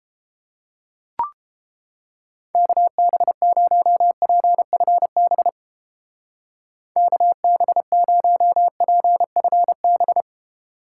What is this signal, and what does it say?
Morse code, heard as a keyed tone, about 29 wpm, 700 Hz. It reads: KB0PFB KB0PFB